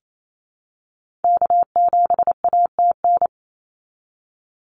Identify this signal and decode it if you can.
Morse code, heard as a keyed tone, about 28 wpm, 700 Hz. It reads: K7ATN
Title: NA SOTA Activator Callsigns 28wpm